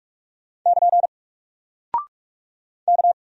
Morse code, heard as a keyed tone, about 44 wpm, 700 Hz. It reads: <KN> K
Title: Single Character (Letter:Number:Punctuation:Prosign) Alt 45wpm